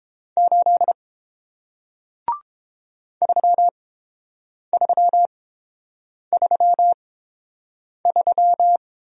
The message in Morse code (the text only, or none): 8 3333